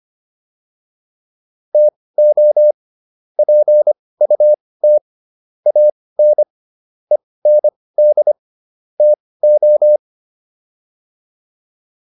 Morse code, heard as a keyed tone, about 25 wpm, spaced as for 18 wpm, 600 Hz. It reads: TO PUT AN END TO